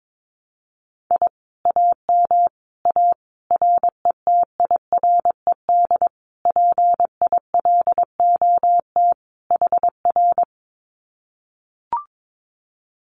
Morse code, heard as a keyed tone, about 22 wpm, 700 Hz. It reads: I AM A RETIRED PILOT HR
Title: Common QSO Phrases Encore 22wpm